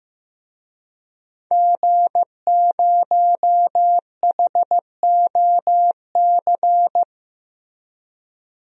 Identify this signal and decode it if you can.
Morse code, heard as a keyed tone, about 15 wpm, 700 Hz. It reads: G0HOC
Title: DXCC-100 Call Signs - 5 Characters - Repeated 3x - 15wpm